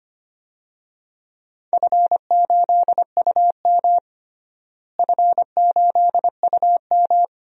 Morse code, heard as a keyed tone, about 25 wpm, 700 Hz. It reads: F8UM F8UM
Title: DXCC-100 Call Signs - 4 Characters - Repeated 2x - 25wpm